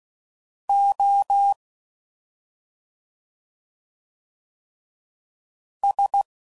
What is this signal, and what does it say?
Morse code, heard as a keyed tone, about 16 wpm, 800 Hz. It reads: O S